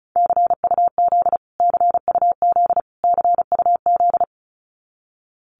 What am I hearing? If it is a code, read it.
Morse code, heard as a keyed tone, about 34 wpm, 700 Hz. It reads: CUZ CUZ CUZ
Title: Abbreviations Repeated 3x - 35wpm